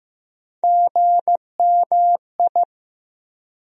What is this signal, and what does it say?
Morse code, heard as a keyed tone, about 16 wpm, 700 Hz. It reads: GMI
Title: USA Call Sign Postfixes - 3 Letters 15wpm